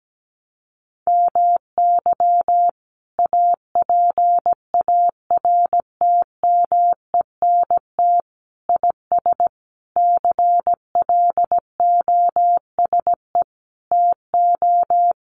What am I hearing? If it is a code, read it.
Morse code, heard as a keyed tone, about 17 wpm, 700 Hz. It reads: MY APARTMENT IS CLOSE TO
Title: Sentences from Top 900 Words 17wpm